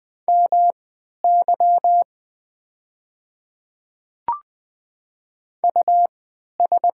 Morse code, heard as a keyed tone, about 20 wpm, spaced as for 11 wpm, 700 Hz. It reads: MY US
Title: 3x Character Spacing - Top 200 Words - 20wpm